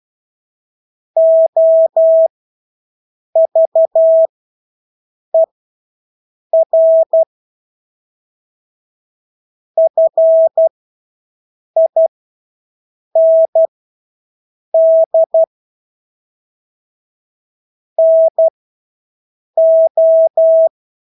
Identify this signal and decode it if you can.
Morse code, heard as a keyed tone, about 12 wpm, spaced as for 6 wpm, 650 Hz. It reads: OVER FIND NO
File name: Engelske ord 12-6 wpm